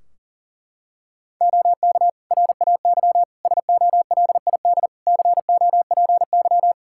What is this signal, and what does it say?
Morse code, heard as a keyed tone, about 40 wpm, 700 Hz. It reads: OK RAY SOLID COPY
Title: Practice QSO - Exchange 3 - More Info - 40wpm